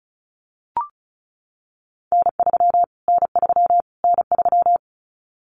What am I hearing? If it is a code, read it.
Morse code, heard as a keyed tone, about 34 wpm, 700 Hz. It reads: N3 N3 N3
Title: Callsign Prefixes for Top 25 DXCC - Repeated 3x - 35wpm